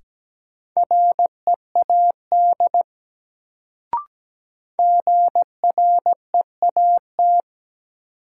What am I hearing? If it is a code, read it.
Morse code, heard as a keyed tone, about 17 wpm, 700 Hz. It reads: READ GREAT
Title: Sentences from Top 200 Words - New Word Drill - 17wpm